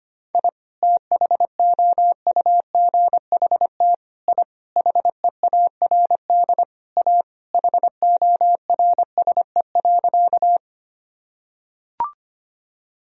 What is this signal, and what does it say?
Morse code, heard as a keyed tone, about 25 wpm, 700 Hz. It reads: I THOUGHT I HEARD A HORSE.
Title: Sentences from Top 1000 Words - Review Part 1 - 25wpm